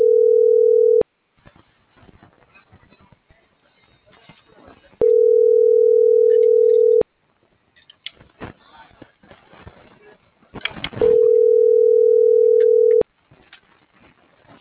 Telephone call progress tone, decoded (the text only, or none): ringback tone